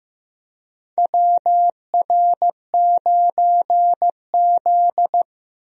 Morse code, heard as a keyed tone, about 15 wpm, 700 Hz. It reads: WR9Z
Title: USA Call Signs - 4 Characters - Repeated 2x - 15wpm